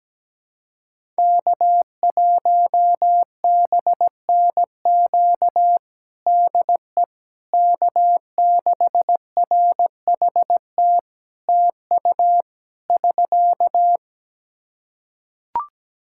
Morse code, heard as a keyed tone, about 17 wpm, 700 Hz. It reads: K1BNQ DE K6RHT TU <SK>
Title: K1USN SST QSO Practice 17wpm